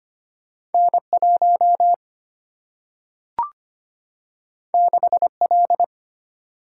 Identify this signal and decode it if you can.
Morse code, heard as a keyed tone, about 25 wpm, 700 Hz. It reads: N 1 6 L
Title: Single-Letter Single-Number 25wpm